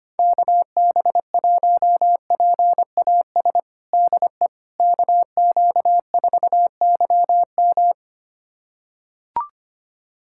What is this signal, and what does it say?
Morse code, heard as a keyed tone, about 25 wpm, 700 Hz. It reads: KB1PAS DE KQ4YM
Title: USA Callsign DE Callsign 25wpm